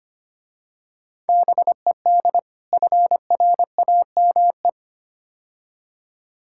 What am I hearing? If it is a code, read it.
Morse code, heard as a keyed tone, about 29 wpm, 700 Hz. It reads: BED FRAME
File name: practice / 25wpm / Difficult Words - Speed-Racing with Hints - 25wpm